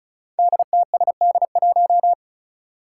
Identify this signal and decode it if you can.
Morse code, heard as a keyed tone, about 34 wpm, 700 Hz. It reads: DTSD1
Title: Five Challenging Letter-Number Combinations 35wpm